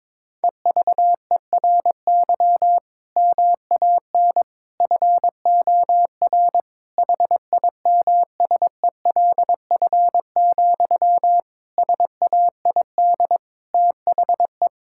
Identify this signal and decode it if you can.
Morse code, heard as a keyed tone, about 21 wpm, 700 Hz. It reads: EVERY MAN FOR HIMSELF, SAID THE
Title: Sentences from Top 1600 Words 22wpm